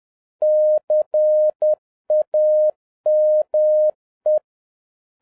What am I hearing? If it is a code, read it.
Morse code, heard as a keyed tone, about 10 wpm, 610 Hz. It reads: CAME